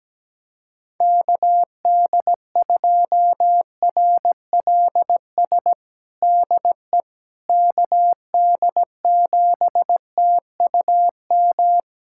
Morse code, heard as a keyed tone, about 17 wpm, 700 Hz. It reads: KD2RLS DE KD7TUM
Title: USA Callsign DE Callsign - 6 Characters - 17wpm